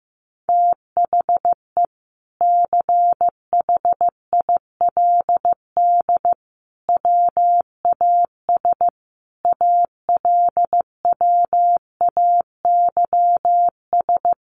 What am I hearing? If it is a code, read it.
Morse code, heard as a keyed tone, about 15 wpm, 700 Hz. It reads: THE CHILD WAS ALWAYS
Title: Sentences from Top 1400 Words 15wpm